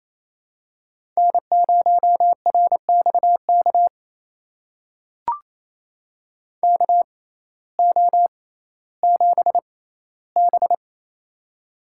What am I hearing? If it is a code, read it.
Morse code, heard as a keyed tone, about 28 wpm, spaced as for 15 wpm, 700 Hz. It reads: N0RXK KO7B
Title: USA Call Signs - 6x Character Spacing - 28wpm